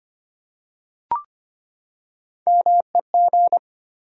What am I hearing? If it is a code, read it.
Morse code, heard as a keyed tone, about 25 wpm, 700 Hz. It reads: MEG
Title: USA Common Names - Max 4 Letters - 25wpm